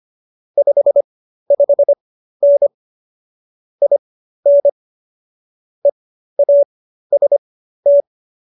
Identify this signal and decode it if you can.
Morse code, heard as a keyed tone, about 25 wpm, spaced as for 13 wpm, 570 Hz. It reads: 55N IN EAST